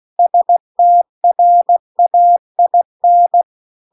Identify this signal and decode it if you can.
Morse code, heard as a keyed tone, about 16 wpm, 700 Hz. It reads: STRAIN
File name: CW-The-Big-Twoheaded-River-2-16WPM